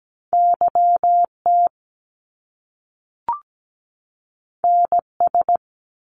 Morse code, heard as a keyed tone, about 17 wpm, 700 Hz. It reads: YT NS
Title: Canadian Provinces and Territories 17wpm